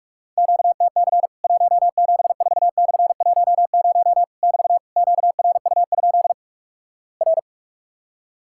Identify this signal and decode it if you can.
Morse code, heard as a keyed tone, about 48 wpm, 700 Hz. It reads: QTC 174/10 <BT> QRU? R
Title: Worked All Europe DX Contest - QTC Exchange - 45wpm